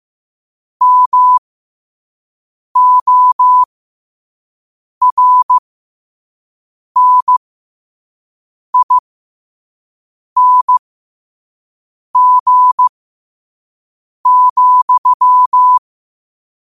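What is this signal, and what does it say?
Morse code, heard as a keyed tone, about 15 wpm, spaced as for 5 wpm, 1000 Hz. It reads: MORNING,
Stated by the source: Morse code practice at 5 wpm.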